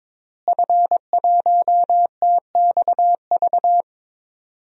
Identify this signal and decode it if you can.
Morse code, heard as a keyed tone, about 22 wpm, 700 Hz. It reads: F1TXV
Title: DXCC-100 Call Signs - 5 Characters - Repeated 2x - 22wpm